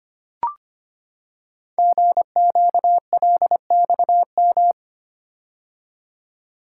Morse code, heard as a keyed tone, about 25 wpm, 700 Hz. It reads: GQLXM
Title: Code Groups - 5 Letters - 25wpm